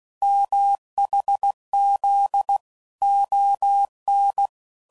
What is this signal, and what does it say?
Morse code, heard as a keyed tone, about 16 wpm, 800 Hz. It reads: MHZ ON
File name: CW Code Groups @15WPM / QSOs / kc7haa